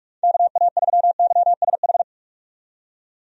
Morse code, heard as a keyed tone, about 46 wpm, 700 Hz. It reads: KA3YSH
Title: USA Call Signs - 6 Characters - 45wpm